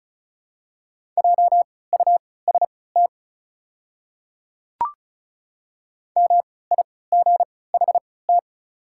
Morse code, heard as a keyed tone, about 36 wpm, spaced as for 20 wpm, 700 Hz. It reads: JUST MIGHT
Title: 3x Character Spacing - Top 500 Words - 35wpm